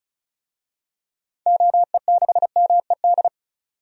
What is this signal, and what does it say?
Morse code, heard as a keyed tone, about 36 wpm, 700 Hz. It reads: OE6MED
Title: DXCC-100 Call Signs - 6 Characters - 35wpm